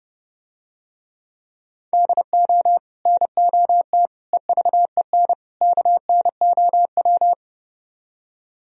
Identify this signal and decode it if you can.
Morse code, heard as a keyed tone, about 30 wpm, 700 Hz. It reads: DO NOT EVEN KNOW